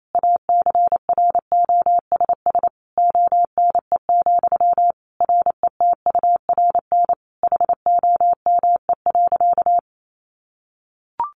Morse code, heard as a keyed tone, about 28 wpm, 700 Hz. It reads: ACROSS ONE, RETURN HOME.
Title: Sentences from Top 800 Words 28wpm